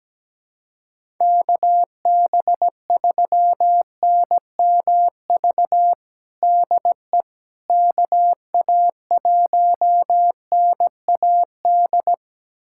Morse code, heard as a keyed tone, about 17 wpm, 700 Hz. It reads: KB3NMV DE KA1NAD
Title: USA Callsign DE Callsign - 6 Characters - 17wpm